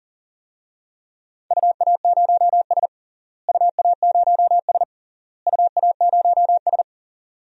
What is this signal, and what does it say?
Morse code, heard as a keyed tone, about 40 wpm, 700 Hz. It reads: UA0S UA0S UA0S
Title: DXCC-100 Call Signs - 4 Characters - Repeated 3x - 40wpm